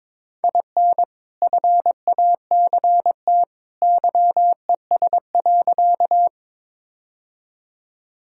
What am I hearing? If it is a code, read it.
Morse code, heard as a keyed tone, about 22 wpm, 700 Hz. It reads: IN FACT YES.
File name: practice / 22wpm / Sentences with 2-Word Phrases - 22wpm